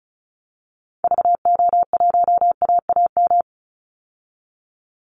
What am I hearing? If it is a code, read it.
Morse code, heard as a keyed tone, about 34 wpm, 700 Hz. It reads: VO1AAM
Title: Call Signs Repeated 3x - 35wpm